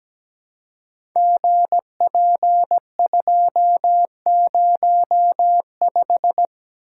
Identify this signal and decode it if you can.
Morse code, heard as a keyed tone, about 17 wpm, 700 Hz. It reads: GP205
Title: Code Groups - 5 Letter:Numbers - 17wpm